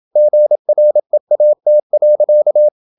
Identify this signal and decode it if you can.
Morse code, heard as a keyed tone, about 27 wpm, 600 Hz. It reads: GREAT.
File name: Never 0027wpm